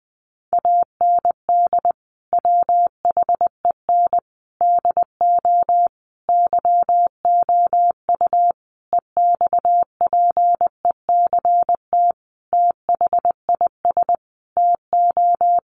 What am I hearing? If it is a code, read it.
Morse code, heard as a keyed tone, about 20 wpm, 700 Hz. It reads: AND WHEN DO YOU EXPECT THIS TO
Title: Sentences from Top 1100 Words 20wpm